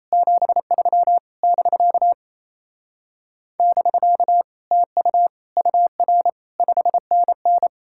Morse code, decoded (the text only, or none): 73 <BK> <BK> TU UR 5NN